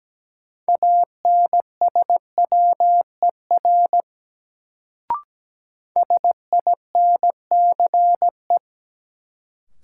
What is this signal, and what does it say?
Morse code, heard as a keyed tone, about 17 wpm, 700 Hz. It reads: ANSWER SINCE
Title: Sentences from Top 300 Words - New Word Drill - 17wpm